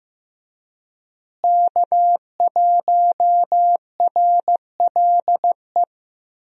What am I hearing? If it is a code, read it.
Morse code, heard as a keyed tone, about 15 wpm, 700 Hz. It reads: K1RLE
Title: USA Call Signs - 5 Characters - Repeated 3x - 15wpm